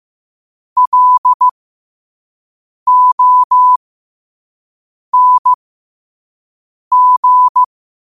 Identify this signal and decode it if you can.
Morse code, heard as a keyed tone, about 15 wpm, spaced as for 5 wpm, 1000 Hz. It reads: LONG
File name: qotdfiles / 5wpm